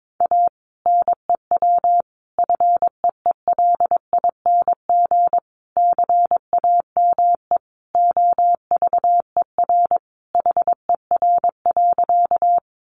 Morse code, heard as a keyed tone, about 22 wpm, 700 Hz. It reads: A NEW FEELING CAME OVER HER.
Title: Sentences from Top 1200 Words 22wpm